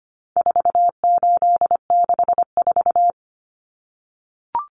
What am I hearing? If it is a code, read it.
Morse code, heard as a keyed tone, about 25 wpm, 700 Hz. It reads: 4864
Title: Numbers - 4-Digit 25wpm